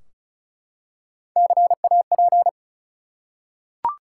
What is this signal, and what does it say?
Morse code, heard as a keyed tone, about 36 wpm, 700 Hz. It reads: CAP
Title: Three Letter Words - 35wpm